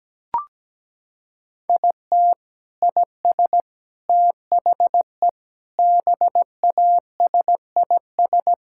Morse code, heard as a keyed tone, about 17 wpm, 700 Hz. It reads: IT IS THE BASIS
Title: Sentences from Top 1000 Words - Review Part 5 - 17wpm